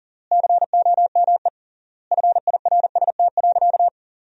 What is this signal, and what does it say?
Morse code, heard as a keyed tone, about 40 wpm, 700 Hz. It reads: COME FIRST.
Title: 3x Word-Spacing - Sentences from Top 300 Words - 40wpm